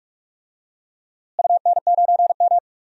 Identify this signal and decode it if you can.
Morse code, heard as a keyed tone, about 46 wpm, 700 Hz. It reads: UN9M
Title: DXCC-100 Call Signs - 4 Characters - Repeated 2x - 45wpm